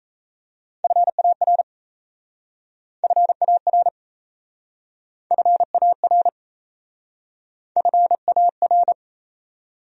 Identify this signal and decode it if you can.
Morse code, heard as a keyed tone, about 37 wpm, 700 Hz. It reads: FAR FAR FAR FAR